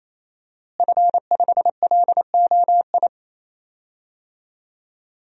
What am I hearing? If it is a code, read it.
Morse code, heard as a keyed tone, about 28 wpm, 700 Hz. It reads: F5LOI
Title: DXCC-100 Call Signs - 5 Characters - Repeated 3x - 28wpm